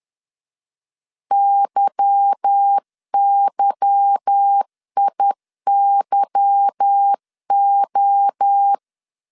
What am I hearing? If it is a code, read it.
Morse code, heard as a keyed tone, about 10 wpm, 800 Hz. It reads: YYIYO
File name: LEZIONI / 8 YJ45 / 8 Y 10wpm 2615